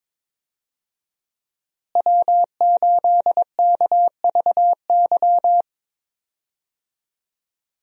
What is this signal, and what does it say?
Morse code, heard as a keyed tone, about 22 wpm, 700 Hz. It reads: W8KVY
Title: USA Call Signs - 5 Characters - Repeated 2x - 22wpm